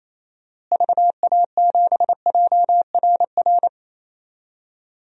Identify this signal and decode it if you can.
Morse code, heard as a keyed tone, about 27 wpm, 700 Hz. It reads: VA7JRR
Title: DXCC-100 Call Signs - 6 Characters - Repeated 3x - 28wpm